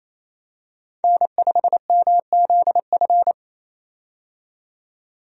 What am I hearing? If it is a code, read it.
Morse code, heard as a keyed tone, about 28 wpm, 700 Hz. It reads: N5MZF
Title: USA Call Signs - 5 Characters - 28wpm